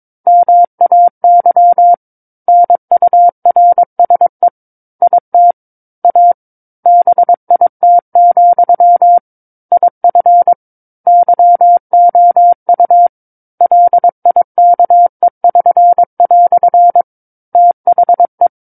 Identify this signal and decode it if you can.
Morse code, heard as a keyed tone, about 22 wpm, 700 Hz. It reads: MAY NURSE IT A BIT, IF YOU LIKE<SN>" THE
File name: 06-Chapter-06 22 WPM